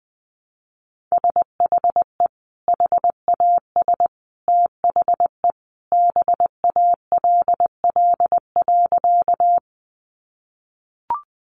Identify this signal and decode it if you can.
Morse code, heard as a keyed tone, about 20 wpm, 700 Hz. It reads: SHE HAS THE BALL.
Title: Sentences from Top 900 Words 20wpm